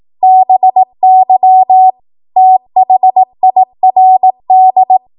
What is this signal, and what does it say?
Morse code, heard as a keyed tone, about 18 wpm, 750 Hz. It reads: BY THIRD